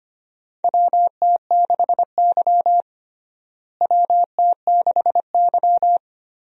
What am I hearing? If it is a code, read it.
Morse code, heard as a keyed tone, about 25 wpm, 700 Hz. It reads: WT6Y WT6Y
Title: USA Call Signs - 4 Characters - Repeated 2x - 25wpm